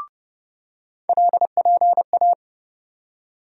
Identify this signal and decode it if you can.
Morse code, heard as a keyed tone, about 30 wpm, 700 Hz. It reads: LPA
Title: USA Call Sign Postfixes - 3 Letters 30wpm